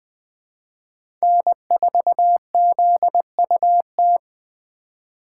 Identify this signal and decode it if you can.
Morse code, heard as a keyed tone, about 20 wpm, 700 Hz. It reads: N4ZUT
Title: USA Call Signs - 5 Characters - 3x Character Spacing - 20wpm